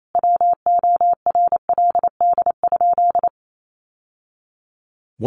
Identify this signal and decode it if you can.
Morse code, heard as a keyed tone, about 28 wpm, 700 Hz. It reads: WORLD?
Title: Sentences from Top 900 Words 28wpm